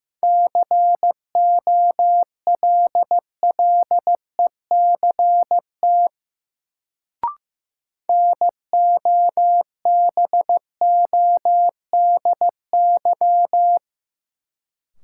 Morse code, heard as a keyed tone, about 15 wpm, 700 Hz. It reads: COLLECT NOBODY
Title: Sentences from Top 1600 Words - New Word Drill - 15wpm